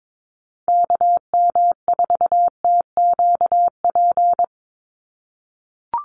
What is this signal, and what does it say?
Morse code, heard as a keyed tone, about 22 wpm, 700 Hz. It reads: KM4TQP
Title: USA Call Signs 22wpm